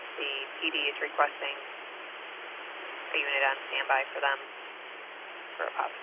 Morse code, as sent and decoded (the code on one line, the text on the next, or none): none